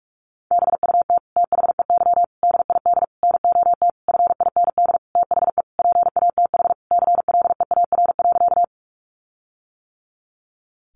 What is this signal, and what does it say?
Morse code, heard as a keyed tone, about 44 wpm, 700 Hz. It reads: BUT THEY DID NOT FIND THE PATH CLEAR.
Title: Sentences from Top 1400 Words 45wpm